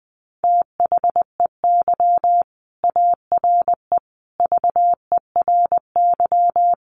Morse code, heard as a keyed tone, about 20 wpm, 700 Hz. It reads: THEY ARE VERY
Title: Sentences from Top 1200 Words 20wpm